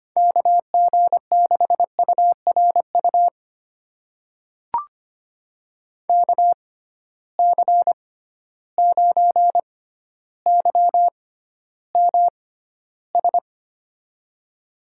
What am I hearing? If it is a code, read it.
Morse code, heard as a keyed tone, about 25 wpm, spaced as for 13 wpm, 700 Hz. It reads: KG6URU KC9YMS
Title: USA Call Signs - 6 Characters - 6x Character Spacing - 25wpm